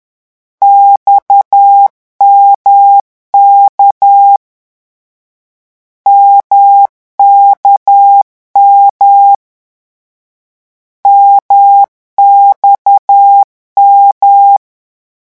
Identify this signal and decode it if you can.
Morse code, heard as a keyed tone, about 10 wpm, 800 Hz. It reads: XMK MKM MXM
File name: morse mp3 / lecon1-vitesse 10